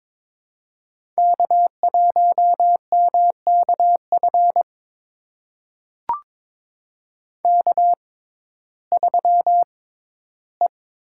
Morse code, heard as a keyed tone, about 22 wpm, spaced as for 13 wpm, 700 Hz. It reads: K1MKF K3E
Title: USA Call Signs - 5 Characters - 6x Character Spacing - 22wpm